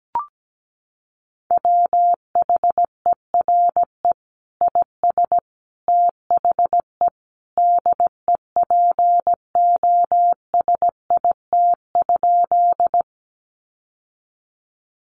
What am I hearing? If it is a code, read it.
Morse code, heard as a keyed tone, about 17 wpm, 700 Hz. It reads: WHERE IS THE DEPOSIT?
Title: Sentences from Top 1300 Words 17wpm